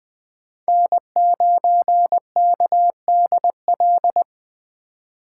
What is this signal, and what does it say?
Morse code, heard as a keyed tone, about 20 wpm, 700 Hz. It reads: N9KDL
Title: USA Call Signs - 5 Characters - 4x Character Spacing - 20wpm